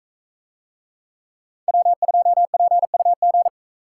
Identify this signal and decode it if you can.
Morse code, heard as a keyed tone, about 44 wpm, 700 Hz. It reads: W2PUG